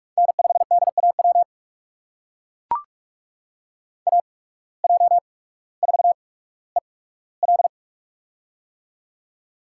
Morse code, heard as a keyed tone, about 41 wpm, spaced as for 19 wpm, 700 Hz. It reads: N5DAW AJ4EL
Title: USA Call Signs - 5 Characters - 8x Character Spacing - 45wpm